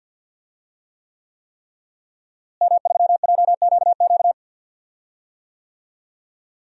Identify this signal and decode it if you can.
Morse code, heard as a keyed tone, about 52 wpm, 700 Hz. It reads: M3JQQ
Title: DXCC-100 Call Signs - 5 Characters - Repeated 3x - 50wpm